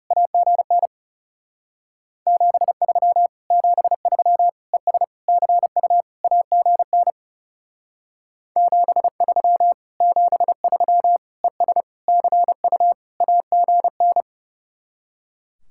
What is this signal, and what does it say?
Morse code, heard as a keyed tone, about 39 wpm, 700 Hz. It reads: AGN 73 73 ES CU AGN 73 73 ES CU AGN